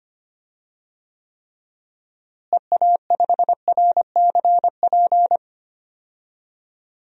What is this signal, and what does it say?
Morse code, heard as a keyed tone, about 25 wpm, 700 Hz. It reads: EA5RCP
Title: DXCC-100 Call Signs - 6 Characters - Repeated 3x - 25wpm